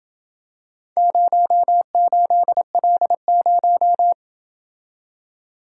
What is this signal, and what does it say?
Morse code, heard as a keyed tone, about 27 wpm, 700 Hz. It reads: 08L0